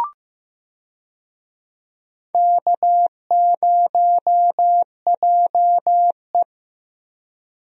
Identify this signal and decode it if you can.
Morse code, heard as a keyed tone, about 15 wpm, 700 Hz. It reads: K0JE
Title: USA Call Signs - 4 Characters - Repeated 2x - 15wpm